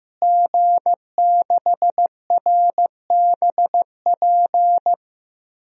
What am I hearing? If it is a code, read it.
Morse code, heard as a keyed tone, about 15 wpm, 700 Hz. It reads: G6RBP
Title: DXCC-100 Call Signs - 5 Characters - Repeated 2x - 15wpm